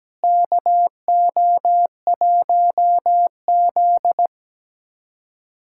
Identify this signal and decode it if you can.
Morse code, heard as a keyed tone, about 17 wpm, 700 Hz. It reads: KO1Z
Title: USA Callsign DE Callsign - 4 Characters - 17wpm